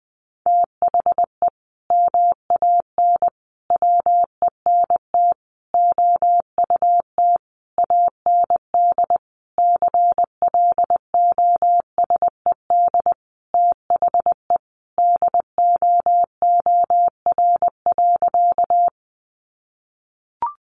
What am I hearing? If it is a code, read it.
Morse code, heard as a keyed tone, about 20 wpm, 700 Hz. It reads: THE MAN WENT OUT AND CLOSED THE DOOR.
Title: Sentences from Top 1400 Words 20wpm